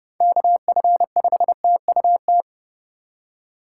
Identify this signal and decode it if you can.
Morse code, heard as a keyed tone, about 32 wpm, 700 Hz. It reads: KF5TUT
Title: USA Call Signs - 6 Characters - 4x Character Spacing - 30wpm